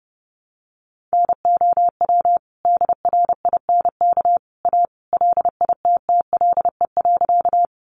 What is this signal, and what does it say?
Morse code, heard as a keyed tone, about 30 wpm, 700 Hz. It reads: NOW DRINK A LITTLE.
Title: Sentences from Top 1500 Words 30wpm